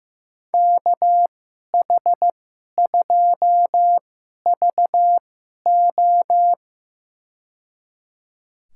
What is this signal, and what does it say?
Morse code, heard as a keyed tone, about 15 wpm, spaced as for 11 wpm, 700 Hz. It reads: KH2VO
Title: USA Call Signs - 5 Characters - 2x Character Spacing - 15wpm